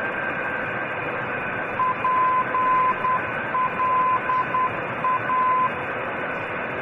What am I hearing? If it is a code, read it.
Morse code, heard as a keyed tone, about 10 wpm, 1020 Hz, with radio noise. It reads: PLA
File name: radio tv historisch / AM Sender / pla